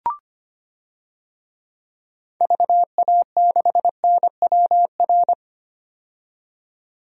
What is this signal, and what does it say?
Morse code, heard as a keyed tone, about 25 wpm, 700 Hz. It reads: VA6NWR
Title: DXCC-100 Call Signs - 6 Characters - 25wpm